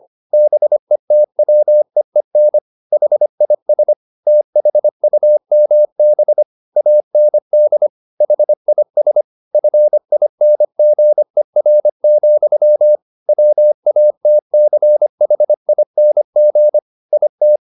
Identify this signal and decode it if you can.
Morse code, heard as a keyed tone, about 25 wpm, 600 Hz. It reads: BETWEEN HIS THUMB AND HIS FINGER, WATCHING IT
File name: the old man and the sea part 1 25wpm